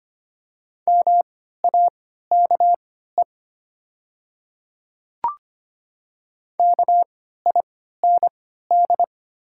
Morse code, heard as a keyed tone, about 25 wpm, spaced as for 14 wpm, 700 Hz. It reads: MAKE KIND
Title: 3x Character Spacing - Top 300 Words - 25wpm